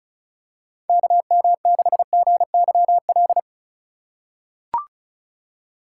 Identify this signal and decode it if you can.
Morse code, heard as a keyed tone, about 36 wpm, 700 Hz. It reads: KM6GYL
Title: USA Call Signs - 6 Characters - 2x Character Spacing - 35wpm